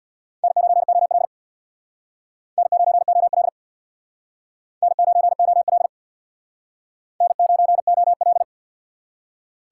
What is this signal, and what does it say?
Morse code, heard as a keyed tone, about 64 wpm, 700 Hz. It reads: N9OL N9OL N9OL N9OL